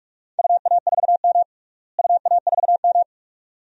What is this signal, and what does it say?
Morse code, heard as a keyed tone, about 46 wpm, 700 Hz. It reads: UA4M UA4M
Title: DXCC-100 Call Signs - 4 Characters - Repeated 2x - 45wpm